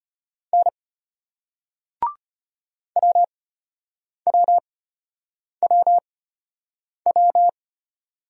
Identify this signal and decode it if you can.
Morse code, heard as a keyed tone, about 32 wpm, spaced as for 8 wpm, 700 Hz. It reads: N WWWW